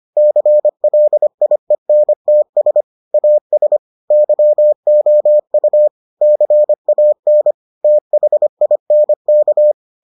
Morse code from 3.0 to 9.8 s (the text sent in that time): AS YOU CAN THINK